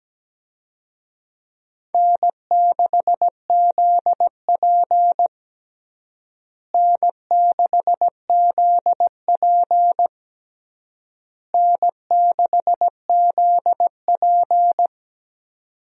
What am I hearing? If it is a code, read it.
Morse code, heard as a keyed tone, about 17 wpm, 700 Hz. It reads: N6ZP N6ZP N6ZP
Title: USA Call Signs - 4 Characters - Repeated 3x - 17wpm